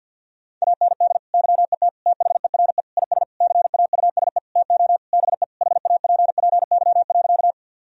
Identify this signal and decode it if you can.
Morse code, heard as a keyed tone, about 51 wpm, 700 Hz. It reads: AND YET THERE IS CAUSE TO BE HAPPY.
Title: Sentences from Top 1000 Words - Review Part 1 - 50wpm